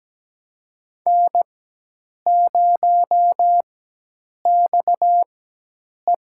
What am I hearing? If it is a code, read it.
Morse code, heard as a keyed tone, about 17 wpm, spaced as for 8 wpm, 700 Hz. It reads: N0XE
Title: USA Call Signs - 4x Character Spacing - 17wpm